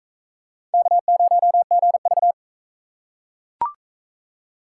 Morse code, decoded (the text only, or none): K0GV